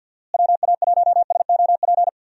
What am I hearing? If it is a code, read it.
Morse code, heard as a keyed tone, about 52 wpm, 700 Hz. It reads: WA1SOP
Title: USA Call Signs - 6 Characters - Repeated 2x - 50wpm